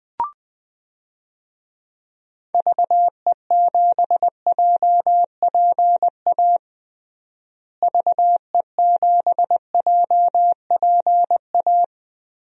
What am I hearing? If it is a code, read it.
Morse code, heard as a keyed tone, about 21 wpm, 700 Hz. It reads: VE7JPA VE7JPA
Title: DXCC-100 Call Signs - 6 Characters - Repeated 2x - 20wpm